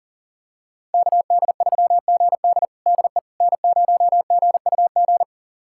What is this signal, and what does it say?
Morse code, heard as a keyed tone, about 40 wpm, 700 Hz. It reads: KD3GD DE N0GUG
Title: USA Callsign DE Callsign - 5 Characters - 40wpm